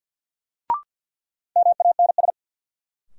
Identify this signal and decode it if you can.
Morse code, heard as a keyed tone, about 49 wpm, 700 Hz. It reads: MANS
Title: Sentences from Top 1000 Words - Review Part 6 - New Word Drill - 50wpm